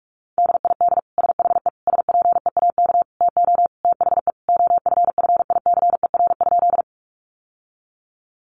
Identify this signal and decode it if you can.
Morse code, heard as a keyed tone, about 44 wpm, 700 Hz. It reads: DID SHE SPEAK TO THE OFFICER?
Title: Sentences from Top 1200 Words 45wpm